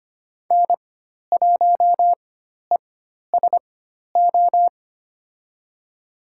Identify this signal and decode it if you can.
Morse code, heard as a keyed tone, about 25 wpm, spaced as for 12 wpm, 700 Hz. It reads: N1ESO
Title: USA Call Signs - 5 Characters - 4x Character Spacing - 25wpm